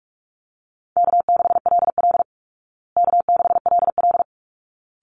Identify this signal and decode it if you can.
Morse code, heard as a keyed tone, about 44 wpm, 700 Hz. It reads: K6LL K6LL
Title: Easy Call Signs Repeated 2x - 45wpm